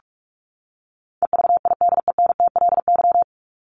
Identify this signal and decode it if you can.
Morse code, heard as a keyed tone, about 44 wpm, 700 Hz. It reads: EVIDENTLY
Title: LY Suffix - Top 100 Words with Base Words 45wpm